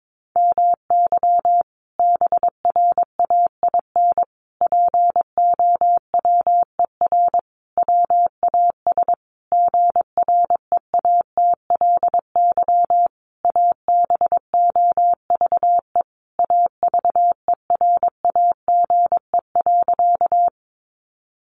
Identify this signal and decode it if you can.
Morse code, heard as a keyed tone, about 22 wpm, 700 Hz. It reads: MY BRAIN POWER WAS GREATLY ABOVE AVERAGE.
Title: Sentences from Top 1400 Words 22wpm